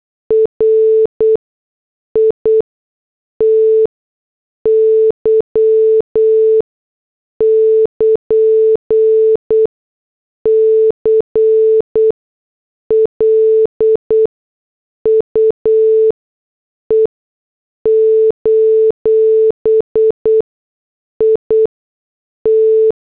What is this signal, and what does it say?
Morse code, heard as a keyed tone, about 8 wpm, spaced as for 6 wpm, 440 Hz. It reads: RITY <KN> CLUE: IT